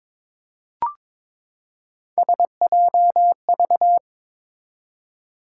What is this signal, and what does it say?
Morse code, heard as a keyed tone, about 21 wpm, 700 Hz. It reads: SJV
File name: ARRL Sections v2 - 22wpm